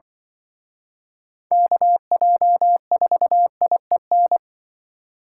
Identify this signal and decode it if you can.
Morse code, heard as a keyed tone, about 24 wpm, 700 Hz. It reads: KJ4IEN
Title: Speed Racing - USA 2x3 Call Signs 20wpm